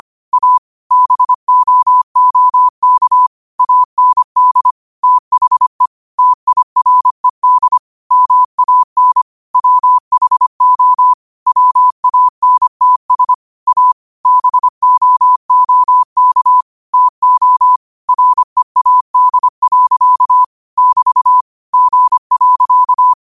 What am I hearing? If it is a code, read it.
Morse code, heard as a keyed tone, about 25 wpm, 1000 Hz. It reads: A BOOK AND THE TIRED MAN WHO WANTS A BOOK TO READ. <BT> G.
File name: qotdfiles / 25wpm